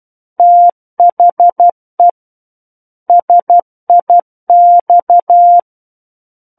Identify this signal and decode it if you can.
Morse code, heard as a keyed tone, about 12 wpm, 700 Hz. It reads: THE SIX
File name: Book One / 03-Chapter 3 - 12 WPM